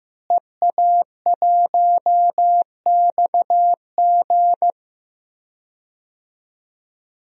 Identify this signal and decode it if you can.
Morse code, heard as a keyed tone, about 15 wpm, 700 Hz. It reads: EA1XG
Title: DXCC-100 Call Signs - 5 Characters - Repeated 2x - 15wpm